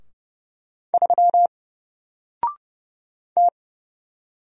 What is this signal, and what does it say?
Morse code, heard as a keyed tone, about 30 wpm, 700 Hz. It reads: 3 T
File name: Single-Letter-Number-30wpm